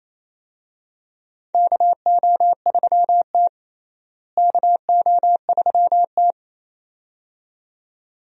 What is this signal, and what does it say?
Morse code, heard as a keyed tone, about 28 wpm, 700 Hz. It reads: KO3T KO3T
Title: USA Call Signs - 4 Characters - Repeated 2x - 28wpm